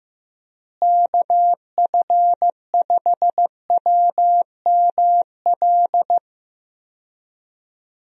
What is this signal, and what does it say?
Morse code, heard as a keyed tone, about 15 wpm, 700 Hz. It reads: KF5WML
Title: USA Call Signs - 6 Characters - Repeated 3x - 15wpm